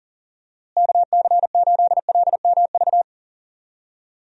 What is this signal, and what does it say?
Morse code, heard as a keyed tone, about 40 wpm, 700 Hz. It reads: KC8LMV
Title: USA Call Signs - 6 Characters - Repeated 2x - 40wpm